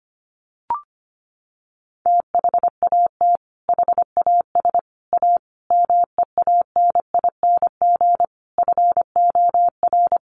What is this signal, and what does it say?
Morse code, heard as a keyed tone, about 25 wpm, 700 Hz. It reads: THAT HAS A MEANING FOR
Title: Sentences from Top 900 Words 25wpm